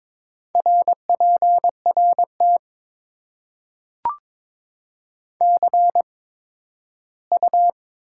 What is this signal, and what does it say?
Morse code, heard as a keyed tone, about 22 wpm, spaced as for 13 wpm, 700 Hz. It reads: RPRT CU
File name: Top 90 QSO Elements - 8x Character Spacing - 22wpm